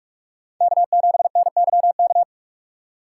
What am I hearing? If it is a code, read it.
Morse code, heard as a keyed tone, about 47 wpm, 700 Hz. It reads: K7NYK
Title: USA Call Signs - 5 Characters - 4x Character Spacing - 45wpm